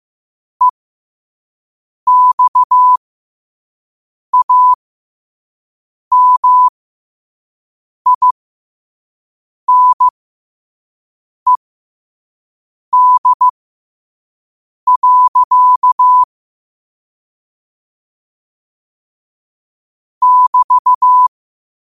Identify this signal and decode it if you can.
Morse code, heard as a keyed tone, about 15 wpm, spaced as for 5 wpm, 1000 Hz. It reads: EXAMINED. <BT>
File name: qotdfiles / 5wpm